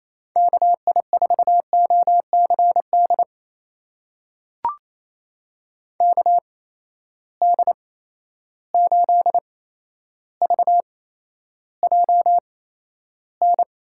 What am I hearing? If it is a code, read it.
Morse code, heard as a keyed tone, about 28 wpm, spaced as for 12 wpm, 700 Hz. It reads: KI4OCD KD8VJN
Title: USA Call Signs - 8x Character Spacing - 28wpm